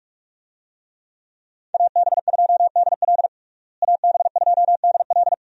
Morse code, heard as a keyed tone, about 41 wpm, 700 Hz. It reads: AB2DL AB2DL
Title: USA Call Signs - 5 Characters - Repeated 2x - 45wpm